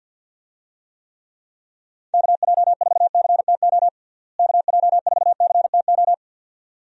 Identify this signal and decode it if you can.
Morse code, heard as a keyed tone, about 49 wpm, 700 Hz. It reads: KJ4CTO KJ4CTO
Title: USA Call Signs - 6 Characters - Repeated 2x - 50wpm